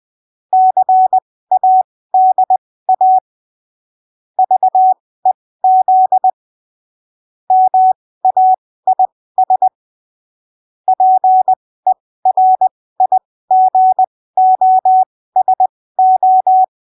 Morse code, heard as a keyed tone, about 20 wpm, spaced as for 15 wpm, 750 Hz. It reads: CADA VEZ MAIS PERIGOSO
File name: arquivos / texto / texto 24 20ppm